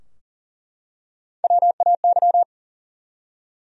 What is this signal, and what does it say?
Morse code, heard as a keyed tone, about 40 wpm, 700 Hz. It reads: WAY
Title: Top 100 Words Repeated 40wpm